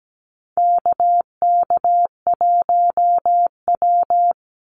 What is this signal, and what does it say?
Morse code, heard as a keyed tone, about 17 wpm, 700 Hz. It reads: KK1W
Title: NA SOTA Activator Callsigns 17wpm